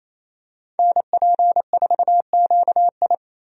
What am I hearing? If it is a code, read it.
Morse code, heard as a keyed tone, about 28 wpm, 700 Hz. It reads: NP4QI
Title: USA Call Signs - 5 Characters - 28wpm